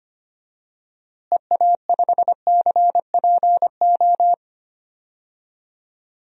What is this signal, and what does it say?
Morse code, heard as a keyed tone, about 25 wpm, 700 Hz. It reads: EA5CPO
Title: DXCC-100 Call Signs - 6 Characters - Repeated 2x - 25wpm